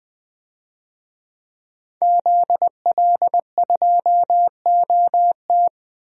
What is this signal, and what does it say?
Morse code, heard as a keyed tone, about 21 wpm, 700 Hz. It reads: ZL2OT
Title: DXCC-100 Call Signs - 5 Characters - 20wpm